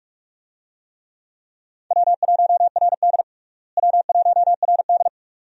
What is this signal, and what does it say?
Morse code, heard as a keyed tone, about 43 wpm, 700 Hz. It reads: W1RD W1RD
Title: USA Call Signs - 4 Characters - Repeated 2x - 45wpm